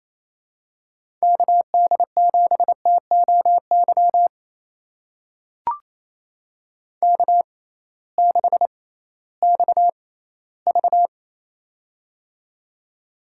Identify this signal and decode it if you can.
Morse code, heard as a keyed tone, about 28 wpm, spaced as for 16 wpm, 700 Hz. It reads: KD7TOY K6XV
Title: USA Call Signs - 6x Character Spacing - 28wpm